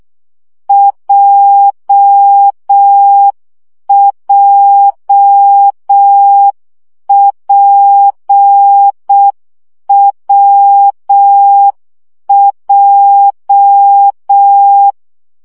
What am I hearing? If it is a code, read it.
Morse code, heard as a keyed tone, about 6 wpm, 800 Hz. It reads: JJPWJ
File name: CW gyakorlas w p j 5 baud